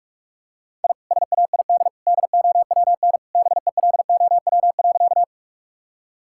Morse code, heard as a keyed tone, about 43 wpm, 700 Hz. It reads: I SAID DOWN BELOW.
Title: Sentences from Top 1000 Words - Review Part 5 - 45wpm